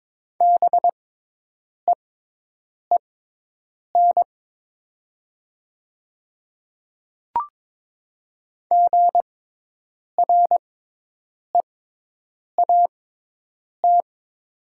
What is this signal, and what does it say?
Morse code, heard as a keyed tone, about 22 wpm, spaced as for 8 wpm, 700 Hz. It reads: BEEN GREAT
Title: 6x Character Spacing - Top 300 Words - 22wpm